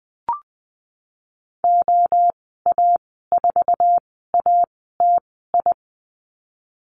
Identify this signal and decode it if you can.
Morse code, heard as a keyed tone, about 20 wpm, spaced as for 15 wpm, 700 Hz. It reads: OA4ATI
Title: 048 Lesson 9g - Number 4 - Call Signs 20wpm